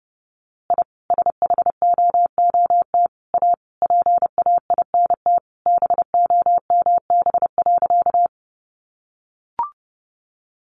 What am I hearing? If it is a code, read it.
Morse code, heard as a keyed tone, about 30 wpm, 700 Hz. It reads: I SHOOT A PAINT BOMB.
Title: A Sound Of Thunder 30wpm